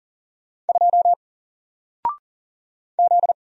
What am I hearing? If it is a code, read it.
Morse code, heard as a keyed tone, about 40 wpm, 700 Hz. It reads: 2 Z
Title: Single Letter-Number Alt 40wpm